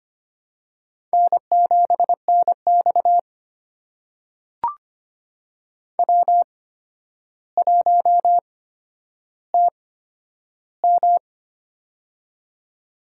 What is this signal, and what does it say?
Morse code, heard as a keyed tone, about 25 wpm, spaced as for 11 wpm, 700 Hz. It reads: N7NX W1TM
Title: USA Call Signs - 4 Characters - 8x Character Spacing - 25wpm